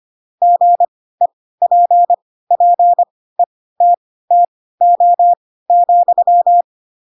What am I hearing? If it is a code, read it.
Morse code, heard as a keyed tone, about 25 wpm, spaced as for 16 wpm, 700 Hz. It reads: GEPPETTO,
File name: Pinocchio / Pinocchio ch7 10 25 ik1hge